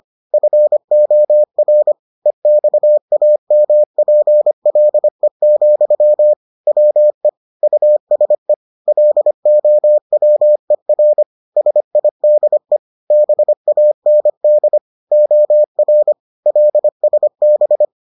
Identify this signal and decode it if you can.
Morse code, heard as a keyed tone, about 25 wpm, 600 Hz. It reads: FOR EXAMPLE, WE USE LOWER SIDE BAND OR LSB